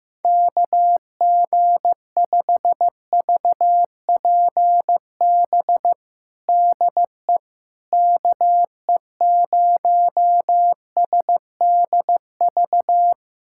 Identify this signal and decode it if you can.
Morse code, heard as a keyed tone, about 16 wpm, 700 Hz. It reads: KG5VPB DE KE0SDV
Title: USA Callsign DE Callsign 15wpm